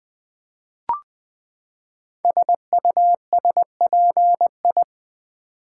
Morse code, heard as a keyed tone, about 20 wpm, 700 Hz. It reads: SUSPI
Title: Build A Word - 5 Letters At A Time - 10-Letter Words - 20wpm